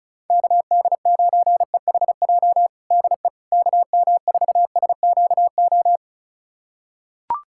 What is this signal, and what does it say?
Morse code, heard as a keyed tone, about 36 wpm, 700 Hz. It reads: KD9EHJ DE KM4SQO
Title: USA Callsign DE Callsign 35wpm